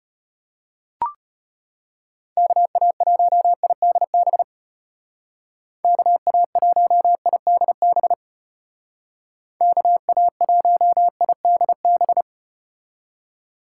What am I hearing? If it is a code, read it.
Morse code, heard as a keyed tone, about 33 wpm, 700 Hz. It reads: KA1IDB KA1IDB KA1IDB